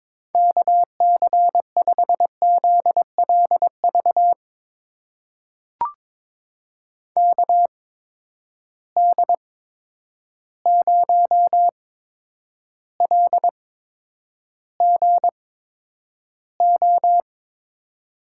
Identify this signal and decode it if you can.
Morse code, heard as a keyed tone, about 22 wpm, spaced as for 9 wpm, 700 Hz. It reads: KC5ZLV KD0LGO
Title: USA Call Signs - 6 Characters - 8x Character Spacing - 22wpm